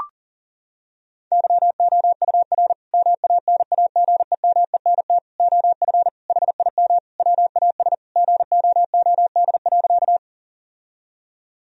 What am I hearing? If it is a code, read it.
Morse code, heard as a keyed tone, about 40 wpm, 700 Hz. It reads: YOUR MANAGEMENT OF HIM WAS GOOD.
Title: Sentences from Top 1000 Words - Review Part 4 - 40wpm